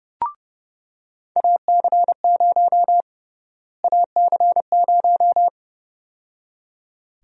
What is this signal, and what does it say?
Morse code, heard as a keyed tone, about 30 wpm, 700 Hz. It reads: AC0 AC0
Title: USA Call Sign Prefixes - Repeated 2x - 30wpm